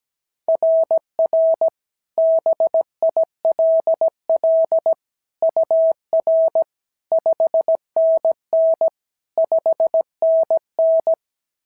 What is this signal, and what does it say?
Morse code, heard as a keyed tone, about 16 wpm, 650 Hz. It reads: RR BILL UR 5NN 5NN
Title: SOTA QSO Practice 17wpm